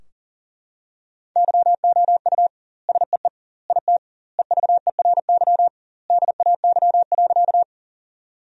Morse code, heard as a keyed tone, about 40 wpm, 700 Hz. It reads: YOU SEE IT EVERY DAY.
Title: 2x Word-Spacing - Sentences from Top 200 Words - 40wpm